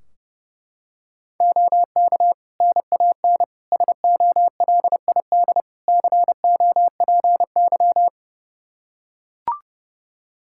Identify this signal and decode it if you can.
Morse code, heard as a keyed tone, about 30 wpm, 700 Hz. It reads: OK NAN SOLID COPY
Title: Practice QSO - Exchange 3 - More Info - 30wpm